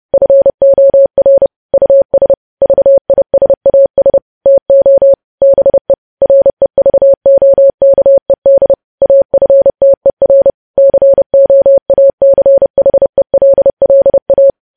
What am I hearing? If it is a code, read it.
Morse code, heard as a keyed tone, about 30 wpm, 550 Hz. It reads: FOR US VISAS TO BE REVOKED AFTER COACHELLA
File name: podcast / News headlines-1x-30x30